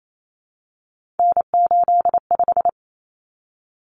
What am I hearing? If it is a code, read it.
Morse code, heard as a keyed tone, about 28 wpm, 700 Hz. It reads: N85
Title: Common Cut Numbers - 3-Digit 28wpm